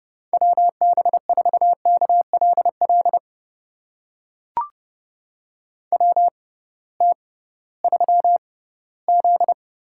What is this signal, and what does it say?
Morse code, heard as a keyed tone, about 32 wpm, spaced as for 18 wpm, 700 Hz. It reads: WB4KLL WT3Z
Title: USA Call Signs - 6x Character Spacing - 30wpm